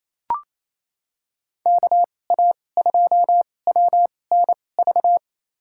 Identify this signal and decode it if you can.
Morse code, heard as a keyed tone, about 28 wpm, spaced as for 20 wpm, 700 Hz. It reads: KA2WNV
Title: USA Call Signs - 6 Characters - 2x Character Spacing - 28wpm